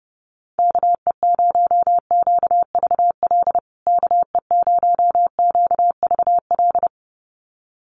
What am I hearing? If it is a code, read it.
Morse code, heard as a keyed tone, about 30 wpm, 700 Hz. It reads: KE0QVL KE0QVL
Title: USA Call Signs Repeated 2x 30wpm